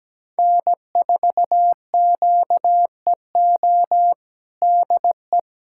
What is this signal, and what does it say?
Morse code, heard as a keyed tone, about 17 wpm, 700 Hz. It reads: N4QEO DE
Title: USA Callsign DE Callsign - 5 Characters - 17wpm